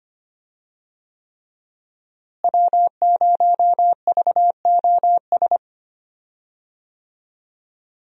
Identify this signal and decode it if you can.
Morse code, heard as a keyed tone, about 25 wpm, 700 Hz. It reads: W0VOS
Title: USA Call Signs - 5 Characters - Repeated 3x - 25wpm